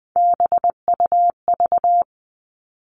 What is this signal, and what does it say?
Morse code, heard as a keyed tone, about 20 wpm, 700 Hz. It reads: BUV
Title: All Cut Numbers - 3-Digit 20wpm